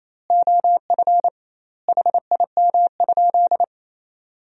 Morse code, heard as a keyed tone, about 28 wpm, 700 Hz. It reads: OF HIM?
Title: 2x Word-Spacing - Sentences from Top 200 Words - 28wpm